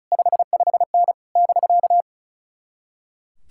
Morse code, heard as a keyed tone, about 34 wpm, 700 Hz. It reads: 55N <BK>
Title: Speed-Racing Common QSO Phrases Encore - 35wpm